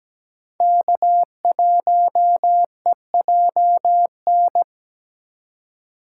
Morse code, heard as a keyed tone, about 17 wpm, 700 Hz. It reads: K1EJN
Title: USA Call Signs - 5 Characters - 8x Character Spacing - 17wpm